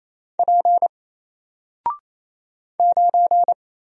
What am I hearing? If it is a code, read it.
Morse code, heard as a keyed tone, about 27 wpm, 700 Hz. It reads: P 9
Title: Single Letter-Number 28wpm